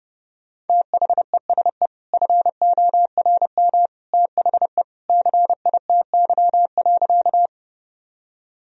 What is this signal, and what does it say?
Morse code, heard as a keyed tone, about 30 wpm, 700 Hz. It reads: THESE FORM THE CITY.
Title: Sentences from Top 1000 Words - Review Part 1 - 30wpm